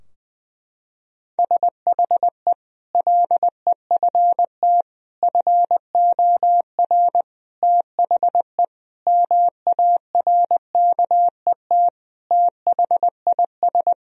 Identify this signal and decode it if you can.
Morse code, heard as a keyed tone, about 20 wpm, 700 Hz. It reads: SHE LEFT FOR THE MARKET THIS
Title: Sentences from Top 1000 Words - Review Part 3 - 20wpm